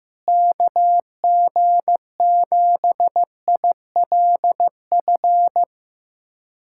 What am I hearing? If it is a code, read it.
Morse code, heard as a keyed tone, about 16 wpm, 700 Hz. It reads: KG7ILF
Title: USA Callsign DE Callsign 15wpm